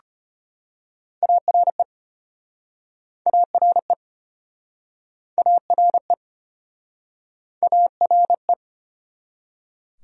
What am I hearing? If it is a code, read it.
Morse code, heard as a keyed tone, about 34 wpm, 700 Hz. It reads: ARE ARE ARE ARE